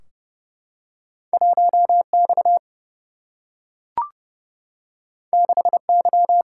Morse code, heard as a keyed tone, about 30 wpm, 700 Hz. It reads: 1X 6Y
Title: Single-Letter Single-Number 30wpm